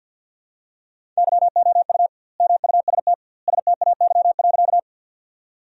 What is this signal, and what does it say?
Morse code, heard as a keyed tone, about 50 wpm, 700 Hz. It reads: YOU MUST STAY.
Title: 2x Word-Spacing - Sentences from Top 500 Words - 50wpm